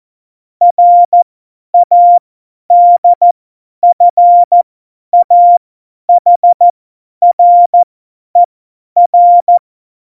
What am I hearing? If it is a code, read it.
Morse code, heard as a keyed tone, about 14 wpm, spaced as for 10 wpm, 700 Hz. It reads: RADFAHRER
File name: Morsekurs / Witze / Witz 05. Februar 2019, 50 F70